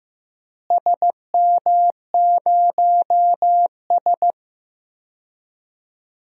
Morse code, heard as a keyed tone, about 15 wpm, 700 Hz. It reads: SM0S
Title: DXCC-100 Call Signs - 4 Characters - Repeated 2x - 15wpm